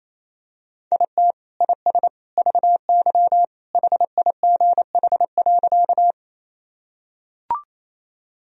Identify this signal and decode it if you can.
Morse code, heard as a keyed tone, about 28 wpm, 700 Hz. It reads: IT IS VY HIGH.
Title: Sentences from Top 200 Words and Abbreviations 28wpm